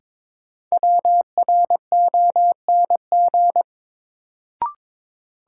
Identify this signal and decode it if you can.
Morse code, heard as a keyed tone, about 22 wpm, 700 Hz. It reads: WRONG
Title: Sets of 5 Words Encore - New Word Drill - 22wpm